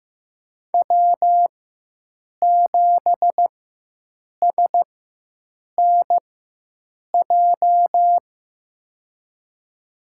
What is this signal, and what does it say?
Morse code, heard as a keyed tone, about 15 wpm, spaced as for 7 wpm, 700 Hz. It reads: W7SNJ
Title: USA Call Signs - 5 Characters - 4x Character Spacing - 15wpm